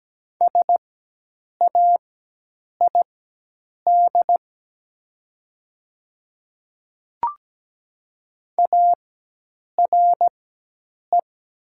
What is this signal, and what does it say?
Morse code, heard as a keyed tone, about 17 wpm, spaced as for 8 wpm, 700 Hz. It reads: SAID ARE
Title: 4x Character Spacing - Top 100 Words - 17wpm